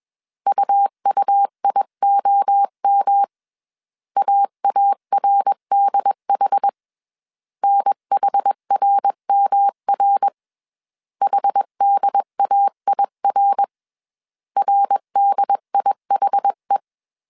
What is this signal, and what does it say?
Morse code, heard as a keyed tone, about 21 wpm, 800 Hz. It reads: UUIOM AARDH NHRMR HDAIR RDIHE